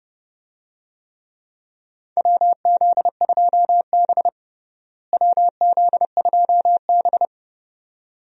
Morse code, heard as a keyed tone, about 30 wpm, 700 Hz. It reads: WZ2B WZ2B
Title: USA Call Signs - 4 Characters - Repeated 2x - 30wpm